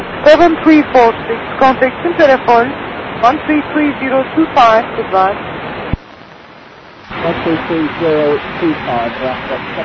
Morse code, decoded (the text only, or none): none